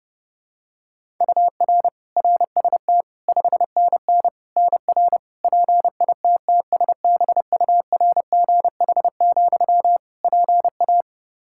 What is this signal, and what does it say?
Morse code, heard as a keyed tone, about 30 wpm, 700 Hz. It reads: UR RST 5NN NR PITTSBURGH, PA
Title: Practice QSO - Exchange 1 - Beginning - 30wpm